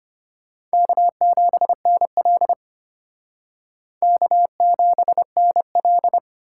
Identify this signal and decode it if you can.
Morse code, heard as a keyed tone, about 32 wpm, 700 Hz. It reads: K7NL K7NL